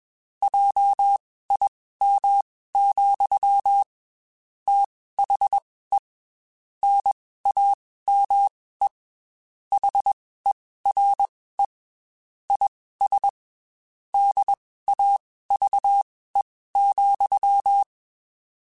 Morse code, heard as a keyed tone, about 21 wpm, spaced as for 15 wpm, 800 Hz. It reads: JIM, THE NAME HERE IS DAVE,
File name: QSOs / QSOs-mp3s / WD0BGP 1@20